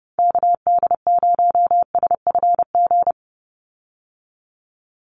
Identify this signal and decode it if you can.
Morse code, heard as a keyed tone, about 30 wpm, 700 Hz. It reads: KD0SFG
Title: USA Call Signs 30wpm